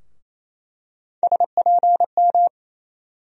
Morse code, heard as a keyed tone, about 28 wpm, 700 Hz. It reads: SPM
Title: USA Call Sign Postfixes - 3 Letters 28wpm